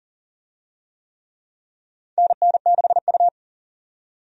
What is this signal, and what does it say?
Morse code, heard as a keyed tone, about 40 wpm, 700 Hz. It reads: NN6U
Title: USA Call Signs - 4 Characters - 40wpm